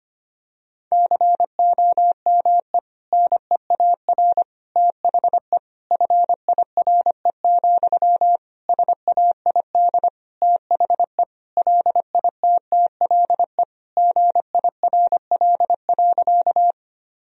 Morse code, heard as a keyed tone, about 25 wpm, 700 Hz. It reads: COME NEAR THE FIRE, SAID THE LITTLE GIRL.
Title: Sentences from Top 1000 Words - Review Part 1 - 25wpm